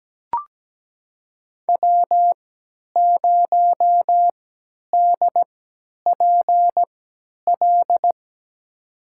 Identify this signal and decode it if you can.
Morse code, heard as a keyed tone, about 17 wpm, spaced as for 10 wpm, 700 Hz. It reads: W0DPL
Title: USA Call Signs - 5 Characters - 3x Character Spacing - 17wpm